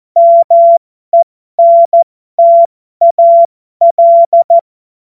Morse code, heard as a keyed tone, about 14 wpm, spaced as for 12 wpm, 680 Hz. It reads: MENTAL
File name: Morsekurs / Witze